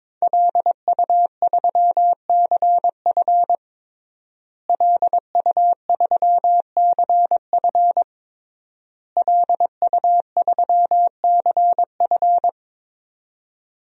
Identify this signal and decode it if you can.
Morse code, heard as a keyed tone, about 22 wpm, 700 Hz. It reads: LU3CF LU3CF LU3CF
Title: DXCC-100 Call Signs - 5 Characters - Repeated 3x - 22wpm